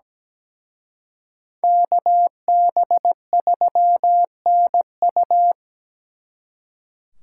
Morse code, heard as a keyed tone, about 20 wpm, 700 Hz. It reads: KB3NU
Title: Speed Racing - USA 2x2 Call Signs 17wpm